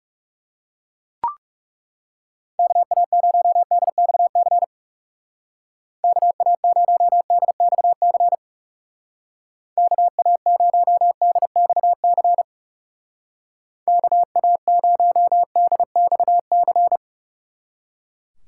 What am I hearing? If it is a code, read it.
Morse code, heard as a keyed tone, about 36 wpm, 700 Hz. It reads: KA0DXC KA0DXC KA0DXC KA0DXC
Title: Speed Racing - USA Call Signs 30wpm